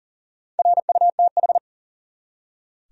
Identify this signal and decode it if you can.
Morse code, heard as a keyed tone, about 40 wpm, 700 Hz. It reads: RUTH
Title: USA Common Names - Max 4 Letters - 40wpm